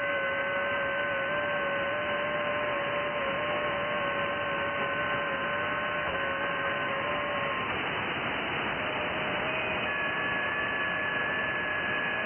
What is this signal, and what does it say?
FT8.
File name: IC-7100